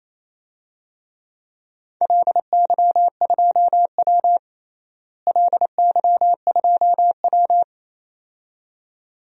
Morse code, heard as a keyed tone, about 27 wpm, 700 Hz. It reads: LY2W LY2W
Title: DXCC-100 Call Signs - 4 Characters - Repeated 2x - 28wpm